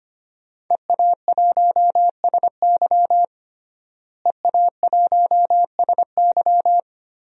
Morse code, heard as a keyed tone, about 25 wpm, 700 Hz. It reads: EA1SY EA1SY
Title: DXCC-100 Call Signs - 5 Characters - Repeated 3x - 25wpm